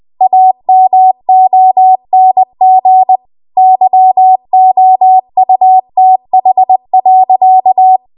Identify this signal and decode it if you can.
Morse code, heard as a keyed tone, about 20 wpm, 750 Hz. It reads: AMONG YOUTH.
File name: Archive / 20 WPM / 20WPM